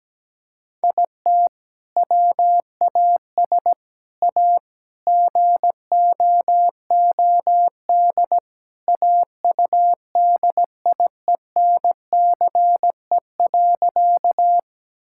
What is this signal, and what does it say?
Morse code, heard as a keyed tone, about 17 wpm, 700 Hz. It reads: IT WAS A GOOD AUDIENCE.
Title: Sentences from Top 1000 Words - Review Part 5 - 17wpm